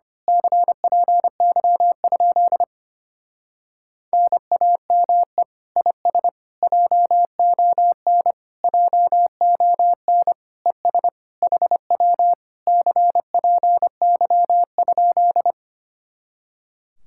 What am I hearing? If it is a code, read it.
Morse code, heard as a keyed tone, about 26 wpm, 700 Hz. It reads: CPY? NAME IS JON JON ES HW CPY?
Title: Speed-Racing Common QSO Phrases - 25wpm